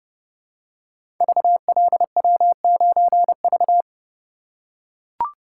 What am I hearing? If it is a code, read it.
Morse code, heard as a keyed tone, about 28 wpm, 700 Hz. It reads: VLW9V
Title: Code Groups - 5 Letter:Numbers - 30wpm